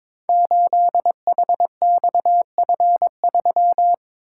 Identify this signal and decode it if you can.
Morse code, heard as a keyed tone, about 22 wpm, 700 Hz. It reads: 8HXF3
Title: Code Groups - 5 Letter:Numbers - 22wpm